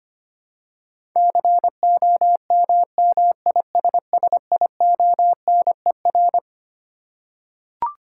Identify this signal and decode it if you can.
Morse code, heard as a keyed tone, about 25 wpm, 700 Hz. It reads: COMMISSIONER
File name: Twelve Letter Words - 25wpm